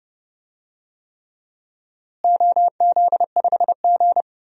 Morse code, heard as a keyed tone, about 32 wpm, 700 Hz. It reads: OZ5G
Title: DXCC-100 Call Signs - 4 Characters - Repeated 3x - 30wpm